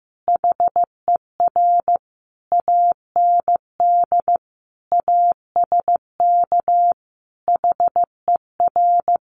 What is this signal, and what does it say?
Morse code, heard as a keyed tone, about 15 wpm, 700 Hz. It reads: HER AND ASK HER
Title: Sentences from Top 1100 Words 15wpm